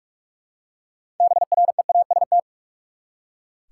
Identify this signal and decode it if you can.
Morse code, heard as a keyed tone, about 46 wpm, 700 Hz. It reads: BREAST
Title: Six Letter Words - 45wpm